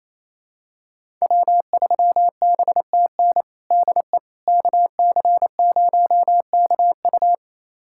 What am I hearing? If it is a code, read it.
Morse code, heard as a keyed tone, about 28 wpm, 700 Hz. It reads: W3BTN DE KC0KU
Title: USA Callsign DE Callsign - 5 Characters - 28wpm